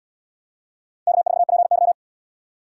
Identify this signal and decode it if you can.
Morse code, heard as a keyed tone, about 68 wpm, 700 Hz. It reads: K4PY